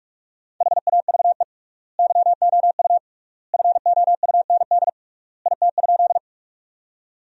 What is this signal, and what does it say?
Morse code, heard as a keyed tone, about 46 wpm, 700 Hz. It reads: HAVE YOU FOUND IT?
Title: 3x Word-Spacing - Sentences from Top 300 Words - 45wpm